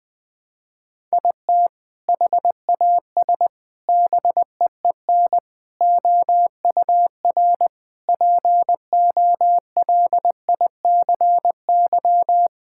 Morse code, heard as a keyed tone, about 20 wpm, 700 Hz. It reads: IT HAS BEEN OUR POLICY
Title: Sentences from Top 1000 Words - Review Part 5 - 20wpm